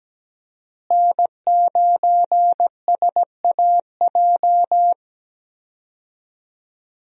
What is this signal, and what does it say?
Morse code, heard as a keyed tone, about 17 wpm, 700 Hz. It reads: N9SAJ
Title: USA Call Signs - 5 Characters - Repeated 3x - 17wpm